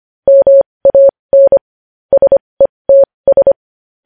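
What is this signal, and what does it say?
Morse code, heard as a keyed tone, about 25 wpm, spaced as for 20 wpm, 550 Hz. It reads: MAN SETS
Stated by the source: Morse code news headlines sent at 20.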